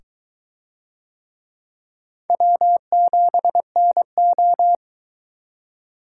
Morse code, heard as a keyed tone, about 22 wpm, 700 Hz. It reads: W7NO